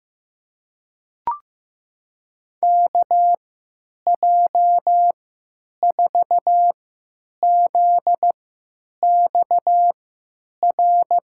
Morse code, heard as a keyed tone, about 15 wpm, spaced as for 9 wpm, 700 Hz. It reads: KJ4ZXR
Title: USA Call Signs - 6 Characters - 3x Character Spacing - 15wpm